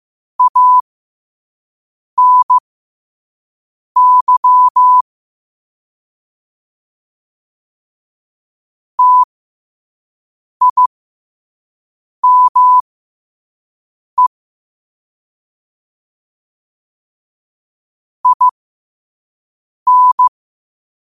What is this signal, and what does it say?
Morse code, heard as a keyed tone, about 15 wpm, spaced as for 5 wpm, 1000 Hz. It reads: ANY TIME IN